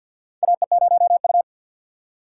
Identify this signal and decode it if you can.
Morse code, heard as a keyed tone, about 53 wpm, 700 Hz. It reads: AE0U
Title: USA Call Signs - 4 Characters - 6x Character Spacing - 50wpm